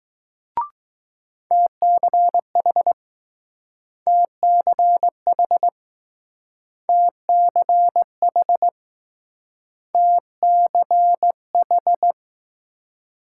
Morse code, heard as a keyed tone, about 20 wpm, 700 Hz. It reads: TCH TCH TCH TCH